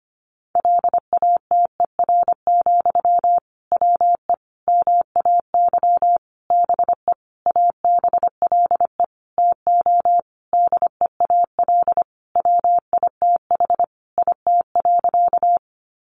Morse code, heard as a keyed tone, about 25 wpm, 700 Hz. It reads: LATER, WE MAY BE ABLE TO DEAL WITH IT.
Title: Sentences from Top 800 Words 25wpm